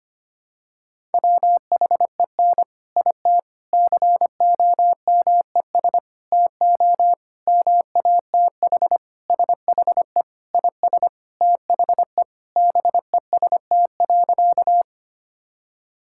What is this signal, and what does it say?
Morse code, heard as a keyed tone, about 25 wpm, 700 Hz. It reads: WHEN IT COMES TO MATH SHE IS THE BEST.
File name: practice / 25wpm / Sentences with 3-Word Phrases - 25wpm